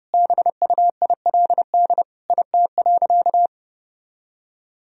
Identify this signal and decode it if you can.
Morse code, heard as a keyed tone, about 30 wpm, 700 Hz. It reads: BUILD IT.
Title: Sentences from Top 1000 Words - Review Part 1 - 30wpm